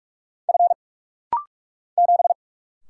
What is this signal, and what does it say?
Morse code, heard as a keyed tone, about 44 wpm, 700 Hz. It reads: F 7
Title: Single Letter-Number Alt - Rapid-Fire 45wpm